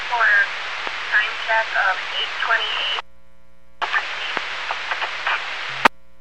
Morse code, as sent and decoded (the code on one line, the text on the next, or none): none